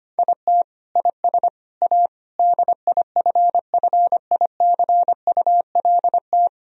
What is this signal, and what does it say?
Morse code, heard as a keyed tone, about 25 wpm, 700 Hz. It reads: IT IS A DIFFICULT
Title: Sentences from Top 1000 Words - Review Part 3 - 25wpm